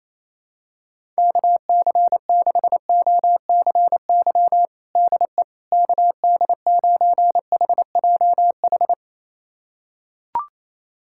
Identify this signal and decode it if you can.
Morse code, heard as a keyed tone, about 28 wpm, 700 Hz. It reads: KC6OCY DE KD9HJH
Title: USA Callsign DE Callsign - 6 Characters - 28wpm